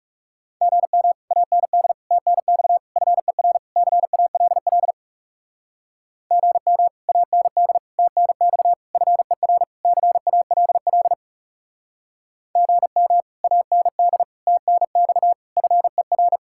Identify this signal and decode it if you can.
Morse code, heard as a keyed tone, about 36 wpm, 700 Hz. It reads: GM AND TNX FER CALL GM AND TNX FER CALL GM AND TNX FER
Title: Speed-Racing Common QSO Phrases - 30wpm